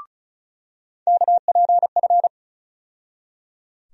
Morse code, heard as a keyed tone, about 36 wpm, 700 Hz. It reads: KPF
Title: USA Call Sign Postfixes - 3 Letters 35wpm